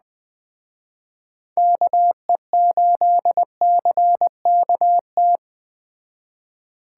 Morse code, heard as a keyed tone, about 20 wpm, 700 Hz. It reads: KE8CKT
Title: Speed Racing - USA Call Signs 17wpm